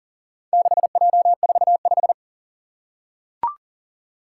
Morse code, heard as a keyed tone, about 40 wpm, 700 Hz. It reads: 6J45
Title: Four Challenging Letter-Number Combinations 40wpm